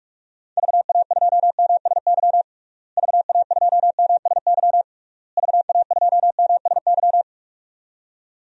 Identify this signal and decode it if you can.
Morse code, heard as a keyed tone, about 46 wpm, 700 Hz. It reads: VA2MSY VA2MSY VA2MSY
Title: DXCC-100 Call Signs - 6 Characters - Repeated 3x - 45wpm